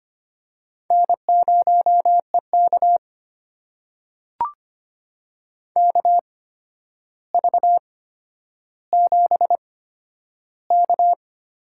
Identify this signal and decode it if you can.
Morse code, heard as a keyed tone, about 25 wpm, spaced as for 11 wpm, 700 Hz. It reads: N0EK KV7K
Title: USA Call Signs - 4 Characters - 8x Character Spacing - 25wpm